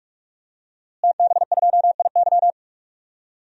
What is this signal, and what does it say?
Morse code, heard as a keyed tone, about 42 wpm, 700 Hz. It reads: TB2IY
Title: DXCC-100 Call Signs - 5 Characters - Repeated 2x - 45wpm